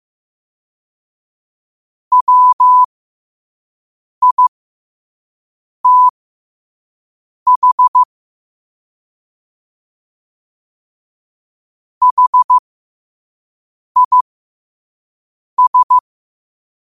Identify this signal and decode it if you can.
Morse code, heard as a keyed tone, about 15 wpm, spaced as for 5 wpm, 1000 Hz. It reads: WITH HIS
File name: qotdfiles / 5wpm